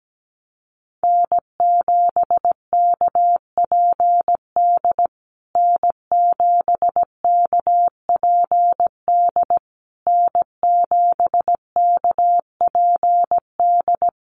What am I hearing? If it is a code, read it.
Morse code, heard as a keyed tone, about 17 wpm, 700 Hz. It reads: N7KPD N7KPD N7KPD
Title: USA Call Signs Repeated 3x 17wpm